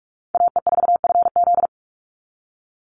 Morse code, heard as a keyed tone, about 44 wpm, 700 Hz. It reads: AE4FZ
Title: NA SOTA Chaser Callsigns 45wpm